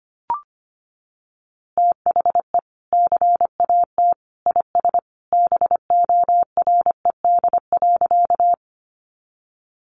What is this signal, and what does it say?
Morse code, heard as a keyed tone, about 25 wpm, 700 Hz. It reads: THE CAT IS BORED.
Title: Easy Phrases 25wpm